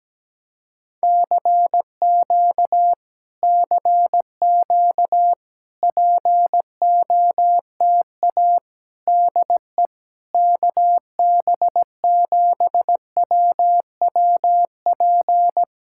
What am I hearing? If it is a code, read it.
Morse code, heard as a keyed tone, about 17 wpm, 700 Hz. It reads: CQ CQ POTA DE KB7WWP
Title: Speed-Racing Common QSO Phrases Encore - 17wpm